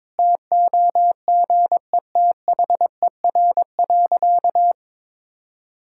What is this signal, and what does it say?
Morse code, heard as a keyed tone, about 22 wpm, 700 Hz. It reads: TOGETHER.
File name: Difficult Sentences with Hints - 22wpm